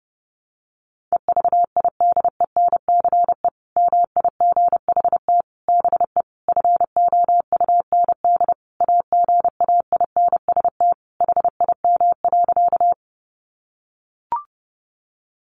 Morse code, heard as a keyed tone, about 28 wpm, 700 Hz. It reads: EVIDENCE MIGHT BE FOUND AGAINST HIM.
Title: Sentences from Top 1300 Words 30wpm